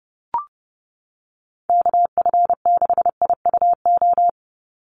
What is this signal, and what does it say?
Morse code, heard as a keyed tone, about 30 wpm, 700 Hz. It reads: KF6IUO
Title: USA Call Signs 30wpm